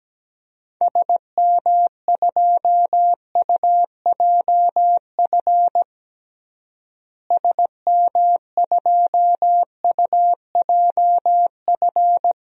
Morse code, heard as a keyed tone, about 17 wpm, 700 Hz. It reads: SM2UJF SM2UJF
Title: DXCC-100 Call Signs - 6 Characters - Repeated 2x - 17wpm